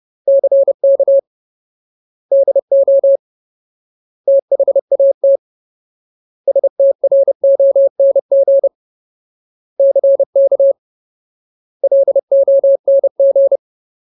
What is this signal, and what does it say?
Morse code, heard as a keyed tone, about 30 wpm, 550 Hz. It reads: CK DO THAT STRONG CK LONG
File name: Ultimate Word List / 30WPM Ultimate